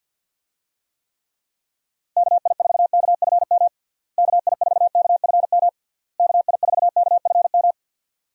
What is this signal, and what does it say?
Morse code, heard as a keyed tone, about 51 wpm, 700 Hz. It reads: KI4KFM KI4KFM KI4KFM
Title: USA Call Signs - 6 Characters - Repeated 3x - 50wpm